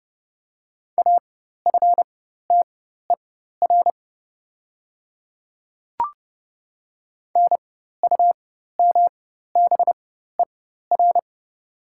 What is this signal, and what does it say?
Morse code, heard as a keyed tone, about 30 wpm, spaced as for 14 wpm, 700 Hz. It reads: AFTER NUMBER
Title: 4x Character Spacing - Top 200 Words - 30wpm